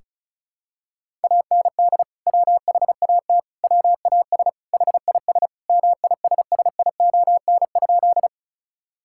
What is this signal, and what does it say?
Morse code, heard as a keyed tone, about 36 wpm, 700 Hz. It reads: AND WHAT WAS HIS MISSION?
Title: Sentences from Top 1600 Words 35wpm